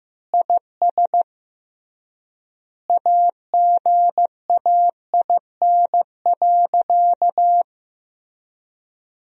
Morse code, heard as a keyed tone, about 15 wpm, 700 Hz. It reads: IS AGAIN.
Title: 3x Word-Spacing - Sentences from Top 200 Words - 15wpm